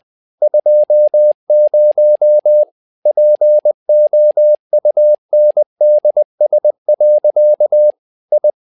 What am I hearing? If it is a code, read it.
Morse code, heard as a keyed tone, about 20 wpm, 600 Hz. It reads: 20 POUNDS. I
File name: tower trouble 20wpm